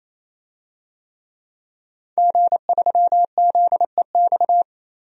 Morse code, heard as a keyed tone, about 28 wpm, 700 Hz. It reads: G3ZEX
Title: DXCC-100 Call Signs - 5 Characters - 28wpm